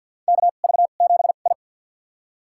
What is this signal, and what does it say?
Morse code, heard as a keyed tone, about 51 wpm, spaced as for 36 wpm, 700 Hz. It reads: KV7I
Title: USA Call Signs - 4 Characters - 2x Character Spacing - 50wpm